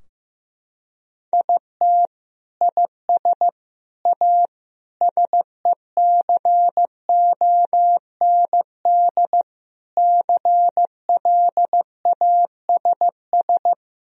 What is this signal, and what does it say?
Morse code, heard as a keyed tone, about 15 wpm, 700 Hz. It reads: IT IS A SECOND CLASS
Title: Sentences from Top 1000 Words - Review Part 2 - 15wpm